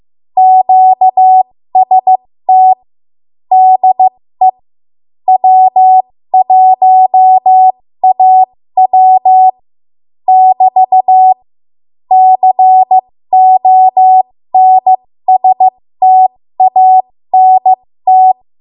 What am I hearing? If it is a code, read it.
Morse code, heard as a keyed tone, about 15 wpm, spaced as for 13 wpm, 750 Hz. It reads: QST DE W1AW <BT> CONSTANT
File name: Morse / 13WPM